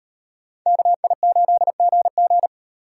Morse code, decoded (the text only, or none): KI8GG